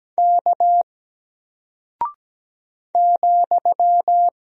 Morse code, heard as a keyed tone, about 17 wpm, 700 Hz. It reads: K ,
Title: Single Character (Letter:Number:Punctuation:Prosign) 17wpm